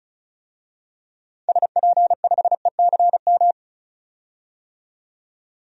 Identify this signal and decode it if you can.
Morse code, heard as a keyed tone, about 36 wpm, 700 Hz. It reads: SP5ECM
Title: DXCC-100 Call Signs - 6 Characters - 35wpm